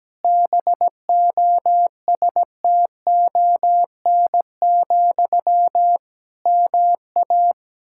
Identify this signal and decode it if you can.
Morse code, heard as a keyed tone, about 17 wpm, 700 Hz. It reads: BOSTON, MA
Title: Practice QSO - Exchange 1 - Beginning - 17wpm